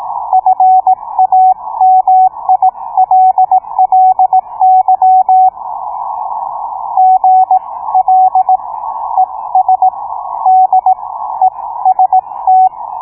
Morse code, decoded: FAMILLY GL ES D EST